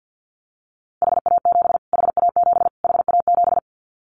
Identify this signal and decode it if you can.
Morse code, heard as a keyed tone, about 50 wpm, 700 Hz. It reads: HA7 HA7 HA7
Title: Callsign Prefixes for Top 25 DXCC - Repeated 3x - 50wpm